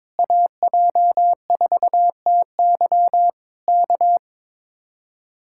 Morse code, heard as a keyed tone, about 21 wpm, 700 Hz. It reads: AJ4TY K
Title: Speed-Racing Common QSO Phrases Encore - 22wpm